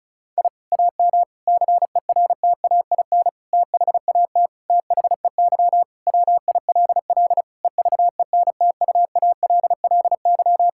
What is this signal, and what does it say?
Morse code, heard as a keyed tone, about 36 wpm, 700 Hz. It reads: I AM CERTAIN THAT THEY WILL EVENTUALLY
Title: Sentences from Top 1000 Words - Review Part 6 - 35wpm